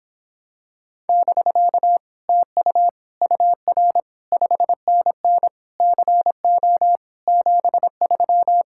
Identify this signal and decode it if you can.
Morse code, heard as a keyed tone, about 26 wpm, 700 Hz. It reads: <BK> TU UR 5NN CO 73